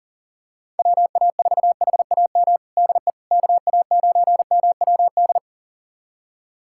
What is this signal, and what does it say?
Morse code, heard as a keyed tone, about 40 wpm, 700 Hz. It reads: WA4HAM DE KA9MWD
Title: USA Callsign DE Callsign - 6 Characters - 40wpm